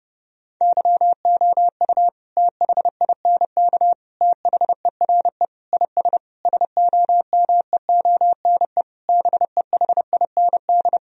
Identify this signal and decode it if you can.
Morse code, heard as a keyed tone, about 32 wpm, 700 Hz. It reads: YOU THINK THERE IS SOMEONE BEHIND
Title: Sentences from Top 1000 Words - Review Part 4 - 30wpm